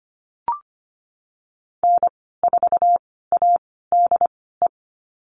Morse code, heard as a keyed tone, about 25 wpm, spaced as for 16 wpm, 700 Hz. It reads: N4ADE
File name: Lesson 15g - Number 5 - Call Signs 25wpm